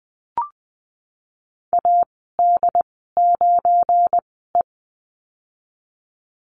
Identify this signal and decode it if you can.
Morse code, heard as a keyed tone, about 19 wpm, spaced as for 14 wpm, 700 Hz. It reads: AD9E
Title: 179 Lesson 29g - Slash - Call Signs 20wpm